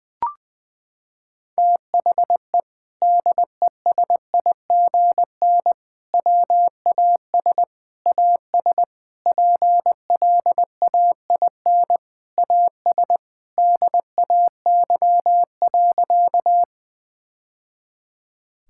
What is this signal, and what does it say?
Morse code, heard as a keyed tone, about 20 wpm, 700 Hz. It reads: THE DESIGN WAS AS PLAIN AS DAY.
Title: Sentences from Top 1000 Words - Review Part 5 - 20wpm